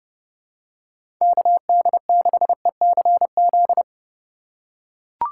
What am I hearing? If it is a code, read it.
Morse code, heard as a keyed tone, about 30 wpm, 700 Hz. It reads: KD6ECZ
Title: USA Call Signs - 6 Characters - 4x Character Spacing - 30wpm